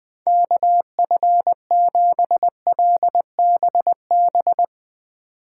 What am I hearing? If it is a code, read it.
Morse code, heard as a keyed tone, about 20 wpm, 700 Hz. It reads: KF7LBB